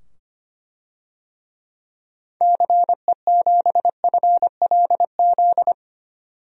Morse code, heard as a keyed tone, about 25 wpm, 700 Hz. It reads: CE7FLZ
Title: DXCC-100 Call Signs - 6 Characters - 25wpm